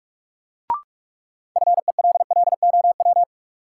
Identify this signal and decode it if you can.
Morse code, heard as a keyed tone, about 46 wpm, 700 Hz. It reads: FELLOW
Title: Sets of 5 Words Encore - New Word Drill - 45wpm